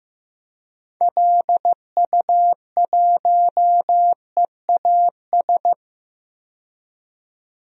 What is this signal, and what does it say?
Morse code, heard as a keyed tone, about 15 wpm, 700 Hz. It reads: LU1EAS
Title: DXCC-100 Call Signs - 6 Characters - 15wpm